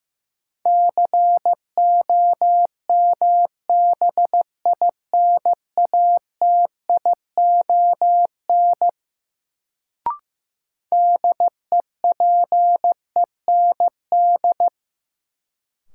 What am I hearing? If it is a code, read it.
Morse code, heard as a keyed tone, about 15 wpm, 700 Hz. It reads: COMBINATION DEPEND
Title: Sentences from Top 1300 Words - New Word Drill Alt - 15wpm